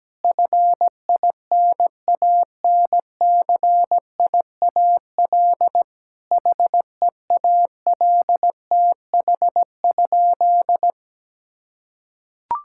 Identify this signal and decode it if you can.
Morse code, heard as a keyed tone, about 17 wpm, 700 Hz. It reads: FINANCIAL HEALTH?
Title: Sentences from Top 1000 Words - Review Part 5 - 17wpm